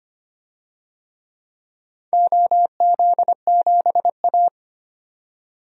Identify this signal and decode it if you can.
Morse code, heard as a keyed tone, about 25 wpm, 700 Hz. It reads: OZ7A
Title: DXCC-100 Call Signs - 4 Characters - 25wpm